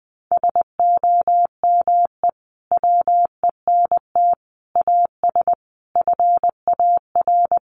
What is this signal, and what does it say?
Morse code, heard as a keyed tone, about 20 wpm, 700 Hz. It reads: SOME WENT AS FAR
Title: Sentences from Top 1100 Words 20wpm